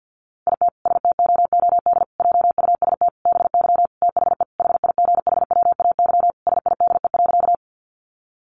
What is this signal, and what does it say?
Morse code, heard as a keyed tone, about 48 wpm, 700 Hz. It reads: IT STOOD JUST BY THE HIGHWAY SIDE.
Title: Sentences from Top 1400 Words 50wpm